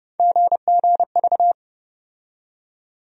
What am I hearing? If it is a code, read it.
Morse code, heard as a keyed tone, about 30 wpm, 700 Hz. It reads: GGV
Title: USA Call Sign Postfixes - 3 Letters 30wpm